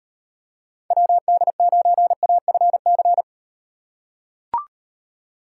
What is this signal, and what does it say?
Morse code, heard as a keyed tone, about 38 wpm, 700 Hz. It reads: WD9AFC